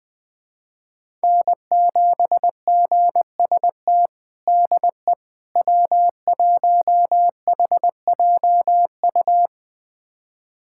Morse code, heard as a keyed tone, about 21 wpm, 700 Hz. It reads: N7GST DE W1HJU
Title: USA Callsign DE Callsign - 5 Characters - 20wpm